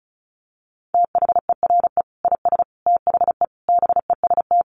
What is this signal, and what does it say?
Morse code, heard as a keyed tone, about 34 wpm, 700 Hz. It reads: THERE IS THE BEST
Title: Sentences from Top 1500 Words 35wpm